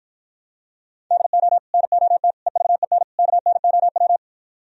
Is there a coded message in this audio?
Yes: Morse code from 1.1 to 4.2 s, heard as a keyed tone, about 52 wpm, 700 Hz. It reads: DO NOT EVEN KNOW